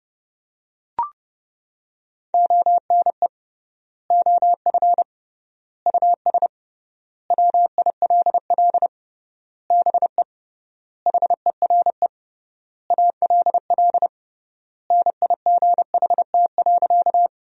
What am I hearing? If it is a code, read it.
Morse code, heard as a keyed tone, about 32 wpm, 700 Hz. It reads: ONE OF US WILL BE HERE ALL NIGHT.
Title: 3x Word-Spacing - Sentences from Top 300 Words - 30wpm